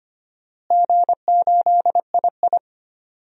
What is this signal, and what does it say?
Morse code, heard as a keyed tone, about 25 wpm, 700 Hz. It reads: G8II
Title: DXCC-100 Call Signs - 4 Characters - Repeated 2x - 25wpm